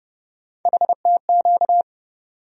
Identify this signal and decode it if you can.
Morse code, heard as a keyed tone, about 30 wpm, 700 Hz. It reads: HTQ
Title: USA Call Sign Postfixes - 3 Letters 30wpm